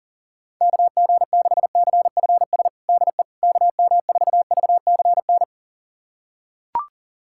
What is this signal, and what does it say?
Morse code, heard as a keyed tone, about 40 wpm, 700 Hz. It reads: KG6CFS DE KM4VCN
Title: USA Callsign DE Callsign - 6 Characters - 40wpm